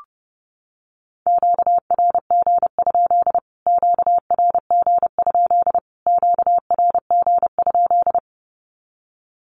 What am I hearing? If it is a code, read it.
Morse code, heard as a keyed tone, about 30 wpm, 700 Hz. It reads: QRG? QRG? QRG?
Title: Q-Codes Repeated 3x 30wpm